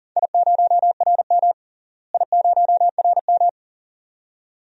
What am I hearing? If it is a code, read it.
Morse code, heard as a keyed tone, about 38 wpm, 700 Hz. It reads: I0RM I0RM
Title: DXCC-100 Call Signs - 4 Characters - Repeated 2x - 40wpm